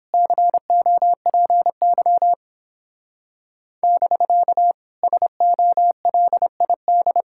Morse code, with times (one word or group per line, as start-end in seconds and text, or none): 0.1-2.3 COPY
3.8-4.7 <BK>
5.0-7.2 SOLID